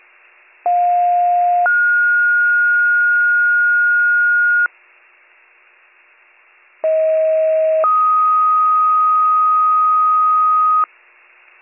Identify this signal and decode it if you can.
Two-tone page, heard as a one-second tone, then a three-second tone, about 710 then 1430 Hz, about 640 then 1250 Hz. It is heard from 0.7 to 10.8 s.